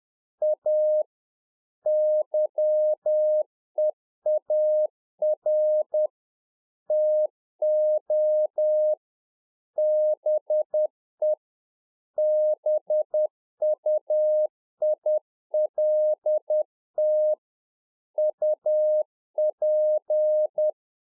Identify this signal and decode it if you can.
Morse code, heard as a keyed tone, about 10 wpm, 610 Hz. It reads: A YEAR TO BE BUILT UP